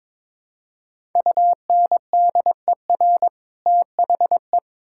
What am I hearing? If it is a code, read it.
Morse code, heard as a keyed tone, about 22 wpm, 700 Hz. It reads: UNDER THE
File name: Build A Phrase - 2-Word Phrases - 22wpm